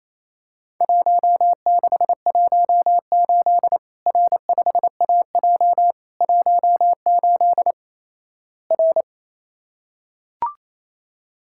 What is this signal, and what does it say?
Morse code, heard as a keyed tone, about 28 wpm, 700 Hz. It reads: 1618 R5AJ 18 R
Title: Worked All Europe DX Contest - QTC Exchange - 28wpm